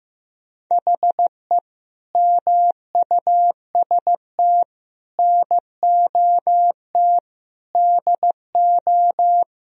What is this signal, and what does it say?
Morse code, heard as a keyed tone, about 15 wpm, 700 Hz. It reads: HE MUST NOT DO
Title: Sentences from Top 1000 Words - Review Part 2 - 15wpm